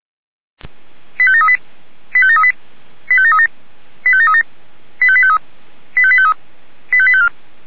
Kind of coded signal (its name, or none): five-tone selective call